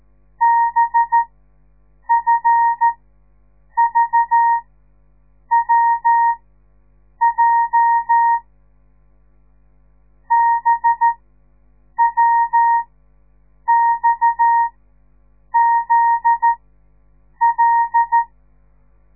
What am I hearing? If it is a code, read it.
Morse code, heard as a keyed tone, about 13 wpm, spaced as for 7 wpm, 930 Hz. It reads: BFVWJ BWXZL
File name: radiomorningstar / morse / les 9